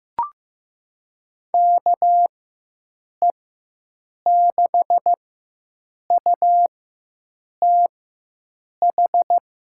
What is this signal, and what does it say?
Morse code, heard as a keyed tone, about 15 wpm, spaced as for 7 wpm, 700 Hz. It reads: KE6UTH
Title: USA Call Signs - 6 Characters - 4x Character Spacing - 15wpm